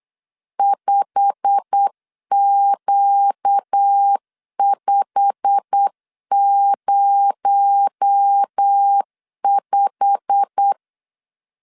Morse code, heard as a keyed tone, about 8 wpm, 800 Hz. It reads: 5Q505
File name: LEZIONI / 8 YJ45 / 8 YJ45 8wpm 3214